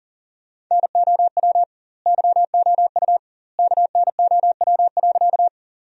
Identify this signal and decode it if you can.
Morse code, heard as a keyed tone, about 40 wpm, 700 Hz. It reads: NOW YOU KNOW.
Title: 2x Word-Spacing - Sentences from Top 100 Words - 40wpm